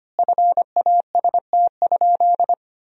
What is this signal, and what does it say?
Morse code, heard as a keyed tone, about 25 wpm, 700 Hz. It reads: FAST?
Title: 4x Word-Spacing - Sentences from Top 500 Words - 25wpm